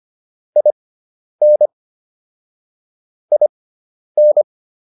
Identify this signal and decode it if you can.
Morse code, heard as a keyed tone, about 25 wpm, spaced as for 10 wpm, 600 Hz. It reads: IN IN